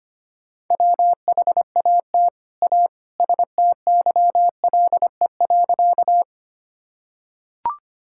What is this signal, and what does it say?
Morse code, heard as a keyed tone, about 25 wpm, 700 Hz. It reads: WHAT A STYLE.
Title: Sentences from Top 1000 Words - Review Part 5 - 25wpm